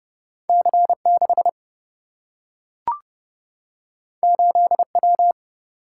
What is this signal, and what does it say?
Morse code, heard as a keyed tone, about 29 wpm, 700 Hz. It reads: C 6 8 W
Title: Single-Letter Single-Number 30wpm